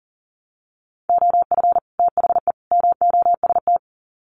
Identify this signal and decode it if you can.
Morse code, heard as a keyed tone, about 40 wpm, 700 Hz. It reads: OF THE MOST
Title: Sets of 3 Words 40wpm